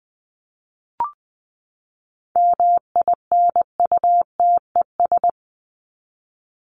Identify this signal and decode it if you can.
Morse code, heard as a keyed tone, about 20 wpm, 700 Hz. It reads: MINUTES
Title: ES:S Suffix - Top 100 Words with Base Words 20wpm